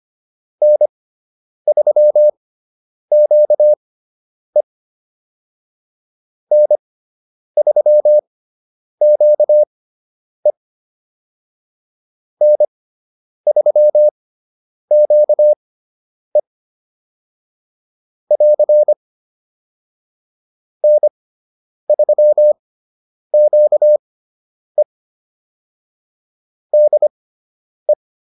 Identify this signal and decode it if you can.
Morse code, heard as a keyed tone, about 25 wpm, spaced as for 9 wpm, 600 Hz. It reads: N3QE N3QE N3QE <AR> N3QE DE